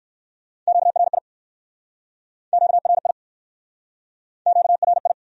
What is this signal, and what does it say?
Morse code, heard as a keyed tone, about 60 wpm, 700 Hz. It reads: QRI QRI QRI